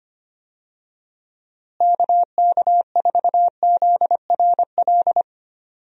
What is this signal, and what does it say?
Morse code, heard as a keyed tone, about 25 wpm, 700 Hz. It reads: KK4ZRL
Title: USA Call Signs - 6 Characters - Repeated 2x - 25wpm